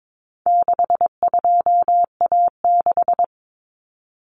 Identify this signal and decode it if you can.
Morse code, heard as a keyed tone, about 22 wpm, 700 Hz. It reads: 62A6
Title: Common Cut Numbers - 4-Digit 22wpm